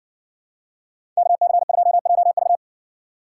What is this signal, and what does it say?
Morse code, heard as a keyed tone, about 60 wpm, 700 Hz. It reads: KC2JV